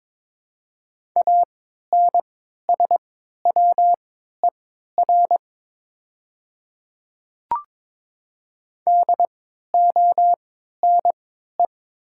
Morse code, heard as a keyed tone, about 22 wpm, spaced as for 12 wpm, 700 Hz. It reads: ANSWER DONE
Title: 3x Character Spacing - Top 500 Words - 22wpm